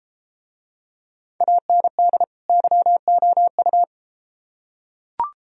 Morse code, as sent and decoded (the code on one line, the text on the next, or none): .- -. -.. / -.-- --- ..-
AND YOU